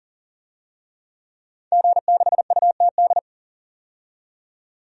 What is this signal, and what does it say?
Morse code, heard as a keyed tone, about 40 wpm, 700 Hz. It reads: G6UTD
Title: DXCC-100 Call Signs - 5 Characters - 40wpm